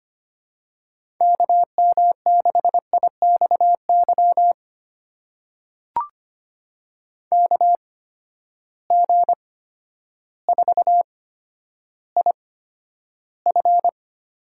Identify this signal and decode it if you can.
Morse code, heard as a keyed tone, about 25 wpm, spaced as for 11 wpm, 700 Hz. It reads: KM6IXY KG4IF
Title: USA Call Signs - 6 Characters - 8x Character Spacing - 25wpm